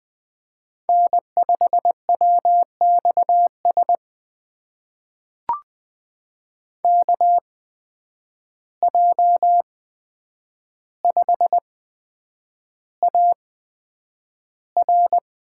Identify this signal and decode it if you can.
Morse code, heard as a keyed tone, about 20 wpm, spaced as for 9 wpm, 700 Hz. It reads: N5WXS KJ5AR
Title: USA Call Signs - 5 Characters - 8x Character Spacing - 20wpm